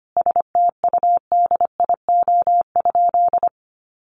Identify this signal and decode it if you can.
Morse code, heard as a keyed tone, about 25 wpm, 700 Hz. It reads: STUDIO?
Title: Sentences from Top 1200 Words 25wpm